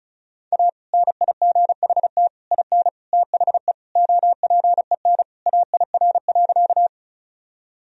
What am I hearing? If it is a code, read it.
Morse code, heard as a keyed tone, about 36 wpm, 700 Hz. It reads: A NIGHT IN THE OPEN AIR.
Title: Sentences from Top 1000 Words - Review Part 1 - 35wpm